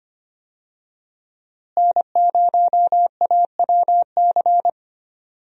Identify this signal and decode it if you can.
Morse code, heard as a keyed tone, about 25 wpm, 700 Hz. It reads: N0AWC
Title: USA Call Signs - 5 Characters - Repeated 2x - 25wpm